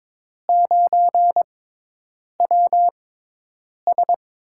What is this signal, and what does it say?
Morse code, heard as a keyed tone, about 22 wpm, spaced as for 8 wpm, 700 Hz. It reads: 9WS
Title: USA Call Signs - 4 Characters - 6x Character Spacing - 22wpm